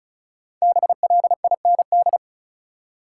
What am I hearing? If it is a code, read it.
Morse code, heard as a keyed tone, about 34 wpm, 700 Hz. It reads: BLIND
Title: Five Letter Words - 35wpm